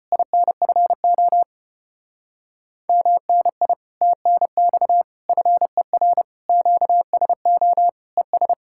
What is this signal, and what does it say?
Morse code, heard as a keyed tone, about 33 wpm, 700 Hz. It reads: INFO MNI TNX FER QSO ES